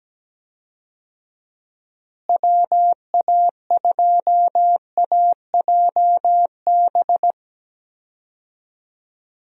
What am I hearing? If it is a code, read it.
Morse code, heard as a keyed tone, about 17 wpm, 700 Hz. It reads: WA2AJB
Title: USA Call Signs - 6 Characters - Repeated 3x - 17wpm